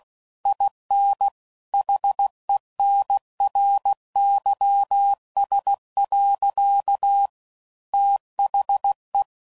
Morse code, heard as a keyed tone, about 16 wpm, 800 Hz. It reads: IN HENRYS. THE
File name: CW Code Groups @15WPM / Technical Subjects / XL